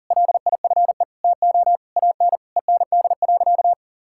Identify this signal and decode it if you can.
Morse code, heard as a keyed tone, about 40 wpm, 700 Hz. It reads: LIFE TO AN END.
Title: Sentences from Top 1000 Words - Review Part 5 - 40wpm